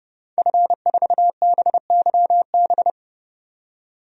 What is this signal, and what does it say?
Morse code, heard as a keyed tone, about 30 wpm, 700 Hz. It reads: F4BYB
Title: DXCC-100 Call Signs - 5 Characters - 30wpm